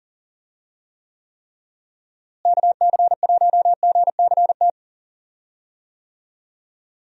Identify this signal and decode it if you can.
Morse code, heard as a keyed tone, about 40 wpm, 700 Hz. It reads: KC1GCT
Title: USA Call Signs - 6 Characters - Repeated 3x - 40wpm